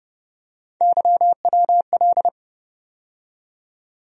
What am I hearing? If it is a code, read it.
Morse code, heard as a keyed tone, about 30 wpm, 700 Hz. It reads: YWL
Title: USA Call Sign Postfixes - 3 Letters 30wpm